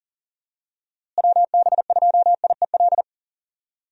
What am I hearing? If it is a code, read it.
Morse code, heard as a keyed tone, about 40 wpm, 700 Hz. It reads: WB2IEL
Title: USA Call Signs - 6 Characters - Repeated 2x - 40wpm